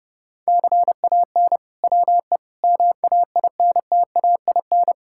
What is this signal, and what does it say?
Morse code, heard as a keyed tone, about 32 wpm, 700 Hz. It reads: CAN WE MAINTAIN
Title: Sentences from Top 1000 Words - Review Part 5 - 30wpm